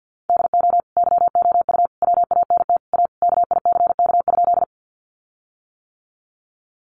Morse code, heard as a keyed tone, about 50 wpm, 700 Hz. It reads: DO YOU WANT A KICK?
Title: Sentences from Top 1300 Words 50wpm